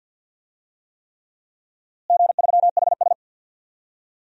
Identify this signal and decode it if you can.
Morse code, heard as a keyed tone, about 51 wpm, 700 Hz. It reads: G3HS
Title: DXCC-100 Call Signs - 4 Characters - Repeated 3x - 50wpm